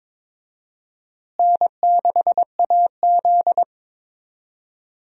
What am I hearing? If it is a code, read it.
Morse code, heard as a keyed tone, about 22 wpm, 700 Hz. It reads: N6AZ
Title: USA Call Signs - 4 Characters - Repeated 2x - 22wpm